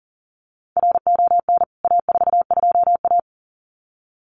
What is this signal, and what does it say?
Morse code, heard as a keyed tone, about 39 wpm, 700 Hz. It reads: RON A42A
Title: CWT Exchange 40wpm